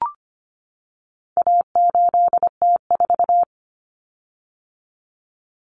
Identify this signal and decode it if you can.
Morse code, heard as a keyed tone, about 25 wpm, 700 Hz. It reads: A8T4
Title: Common Cut Numbers - 4-Digit 25wpm